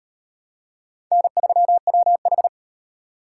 Morse code, heard as a keyed tone, about 38 wpm, 700 Hz. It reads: N3WH